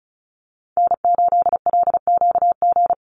Morse code, heard as a keyed tone, about 34 wpm, 700 Hz. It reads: N8LQG
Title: USA Call Signs 35wpm